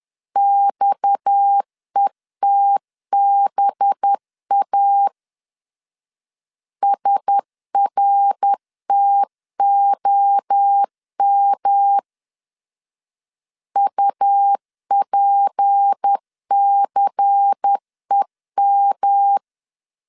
Morse code, heard as a keyed tone, about 10 wpm, 800 Hz. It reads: XETBA SRTOM UPCEM